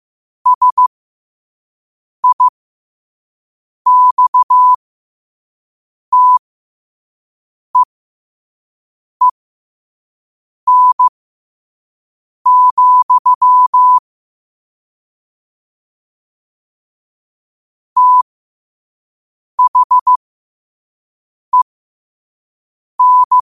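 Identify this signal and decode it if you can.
Morse code, heard as a keyed tone, about 15 wpm, spaced as for 5 wpm, 1000 Hz. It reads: SIXTEEN, THEN
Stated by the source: Morse quote-of-the-day practice at 5 wpm.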